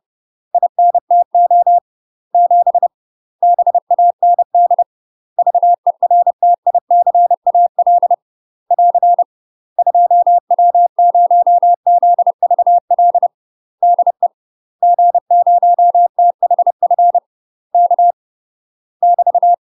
Morse code, heard as a keyed tone, about 30 wpm, 700 Hz. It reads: INTO 7 BAND VERTICAL <AR> 2W0ZVL DE G0THF K <BT>
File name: UK MASTER QSOs TRAINING / 30WPM / UK MASTER QSOs 0048